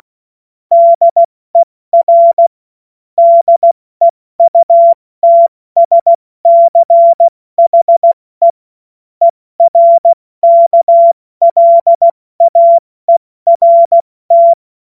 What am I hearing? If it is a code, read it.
Morse code, heard as a keyed tone, about 16 wpm, 680 Hz. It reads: DER DEUTSCHE ERKLAERT
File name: Morsekurs / Witze / Witz 07.November 2023 70 F80